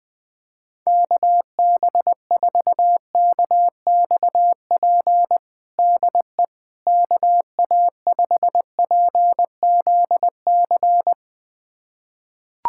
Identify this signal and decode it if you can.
Morse code, heard as a keyed tone, about 20 wpm, 700 Hz. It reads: KB4KXP DE KA5PZC
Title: USA Callsign DE Callsign - 6 Characters - 20wpm